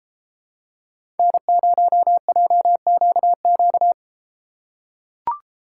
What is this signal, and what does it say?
Morse code, heard as a keyed tone, about 31 wpm, 700 Hz. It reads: N0JQQ